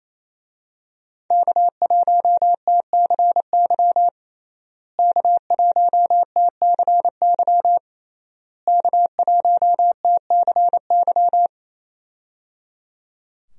Morse code, heard as a keyed tone, about 27 wpm, 700 Hz. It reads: K1TCY K1TCY K1TCY
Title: USA Call Signs - 5 Characters - Repeated 3x - 28wpm